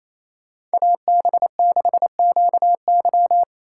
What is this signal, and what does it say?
Morse code, heard as a keyed tone, about 32 wpm, 700 Hz. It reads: AB6QY
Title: Speed Racing - USA Call Signs 28wpm